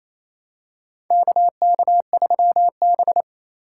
Morse code, heard as a keyed tone, about 28 wpm, 700 Hz. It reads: KK3B
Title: USA Call Signs - 4 Characters - 2x Character Spacing - 28wpm